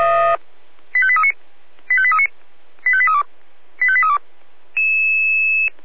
Five-tone selective call (ZVEI), heard as quick runs of five tones, tones of about 70 ms. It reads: 85839 85839 85832 85832